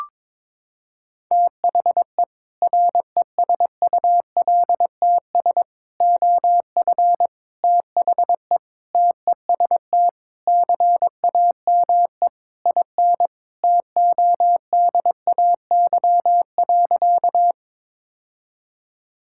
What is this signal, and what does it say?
Morse code, heard as a keyed tone, about 21 wpm, 700 Hz. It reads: THE RESULTS OF THE TEST CAME IN TODAY.
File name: practice / 22wpm / Sentences with 4-Word Phrases - 22wpm